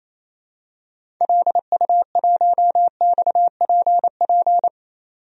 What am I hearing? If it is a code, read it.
Morse code, heard as a keyed tone, about 27 wpm, 700 Hz. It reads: LU1XPP
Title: DXCC-100 Call Signs - 6 Characters - 28wpm